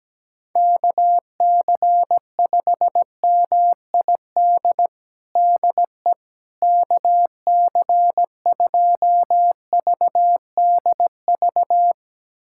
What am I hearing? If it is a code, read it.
Morse code, heard as a keyed tone, about 17 wpm, 700 Hz. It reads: KC5MID DE KC2VDV
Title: USA Callsign DE Callsign - 6 Characters - 17wpm